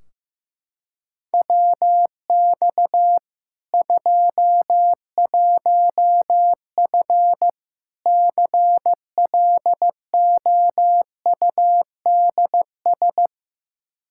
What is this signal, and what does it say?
Morse code, heard as a keyed tone, about 15 wpm, 700 Hz. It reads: WX 21F CLOUDS
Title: K1USN SST QSO Practice 15wpm